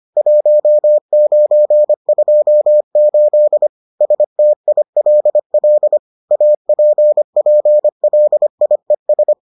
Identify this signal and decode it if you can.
Morse code, heard as a keyed tone, about 25 wpm, 600 Hz. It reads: 1928 STILL APPLIES